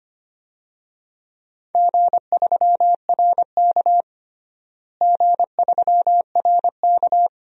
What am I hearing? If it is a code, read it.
Morse code, heard as a keyed tone, about 25 wpm, 700 Hz. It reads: G3RK G3RK
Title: DXCC-100 Call Signs - 4 Characters - Repeated 2x - 25wpm